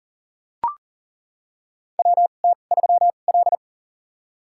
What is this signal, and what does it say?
Morse code, heard as a keyed tone, about 40 wpm, spaced as for 29 wpm, 700 Hz. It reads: WT3L
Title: USA Call Signs - 2x Character Spacing - 40wpm